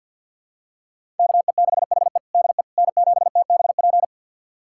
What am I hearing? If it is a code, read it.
Morse code, heard as a keyed tone, about 51 wpm, 700 Hz. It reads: KE6HE DE N7TBP
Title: USA Callsign DE Callsign - 5 Characters - 50wpm